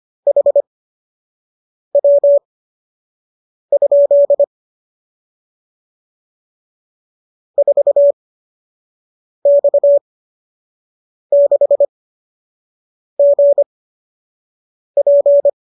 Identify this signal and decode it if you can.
Morse code, heard as a keyed tone, about 25 wpm, spaced as for 6 wpm, 570 Hz. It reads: HW? 4X6GP